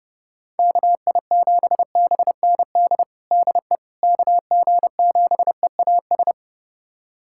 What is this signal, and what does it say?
Morse code, heard as a keyed tone, about 30 wpm, 700 Hz. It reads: KI7BND DE KG7EAS
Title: USA Callsign DE Callsign - 6 Characters - 30wpm